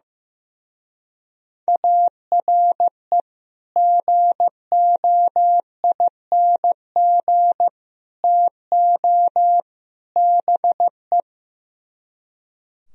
Morse code, heard as a keyed tone, about 15 wpm, 700 Hz. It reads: ARE GOING TO BE
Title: Speed-Racing - Sets of 4 Words 15wpm